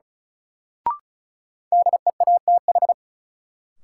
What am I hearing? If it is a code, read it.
Morse code, heard as a keyed tone, about 36 wpm, 700 Hz. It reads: DEATH
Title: Sentences from Top 1100 Words - New Word Drill - 35wpm